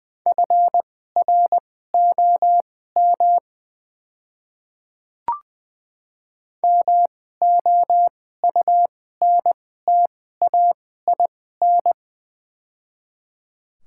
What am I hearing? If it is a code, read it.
Morse code, heard as a keyed tone, about 20 wpm, spaced as for 14 wpm, 700 Hz. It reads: FROM MOUNTAIN
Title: 2x Character Spacing - Top 500 Words - 20wpm